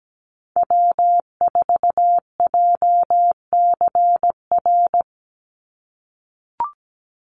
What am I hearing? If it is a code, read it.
Morse code, heard as a keyed tone, about 17 wpm, 700 Hz. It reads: W4JCR
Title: USA Call Signs 17wpm